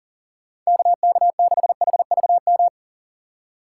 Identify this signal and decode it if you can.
Morse code, heard as a keyed tone, about 40 wpm, 700 Hz. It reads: KK6HVM
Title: USA Call Signs - 6 Characters - 6x Character Spacing - 40wpm